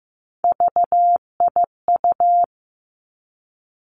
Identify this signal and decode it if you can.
Morse code, heard as a keyed tone, about 15 wpm, 700 Hz. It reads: VIU
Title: Three Challenging Letter-Number Combinations 15wpm